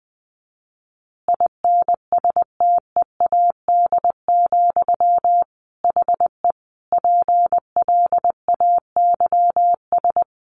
Morse code, heard as a keyed tone, about 20 wpm, 700 Hz. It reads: INSTEAD, HE PLAYS
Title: Additional Easy Phrases 20wpm